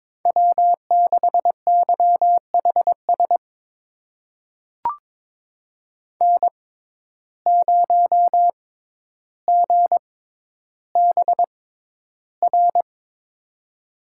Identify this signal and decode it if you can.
Morse code, heard as a keyed tone, about 22 wpm, spaced as for 11 wpm, 700 Hz. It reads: W6YHS N0GBR
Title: USA Call Signs - 5 Characters - 6x Character Spacing - 22wpm